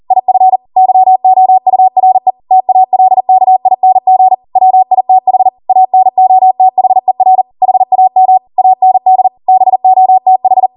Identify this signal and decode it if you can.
Morse code, heard as a keyed tone, about 40 wpm, 750 Hz. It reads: IF YOURE TALKING WITH ANOTHER HAM AND BOTH